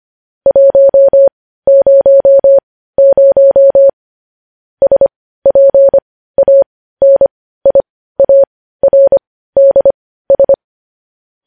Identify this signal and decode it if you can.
Morse code, heard as a keyed tone, about 25 wpm, spaced as for 15 wpm, 550 Hz. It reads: 100 SPANIARDS
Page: news headlines in morse code at 15 wpm